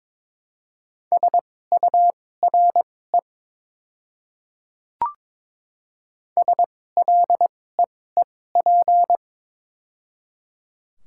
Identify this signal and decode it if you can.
Morse code, heard as a keyed tone, about 22 wpm, spaced as for 16 wpm, 700 Hz. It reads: SURE SLEEP
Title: 2x Character Spacing - Top 500 Words - 22wpm